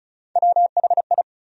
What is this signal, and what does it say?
Morse code, heard as a keyed tone, about 34 wpm, 700 Hz. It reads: WHI
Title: USA Call Sign Postfixes - 3 Letters 35wpm